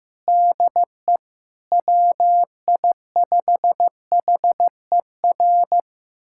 Morse code, heard as a keyed tone, about 15 wpm, 700 Hz. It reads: DE WI5HER
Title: USA Callsign DE Callsign 15wpm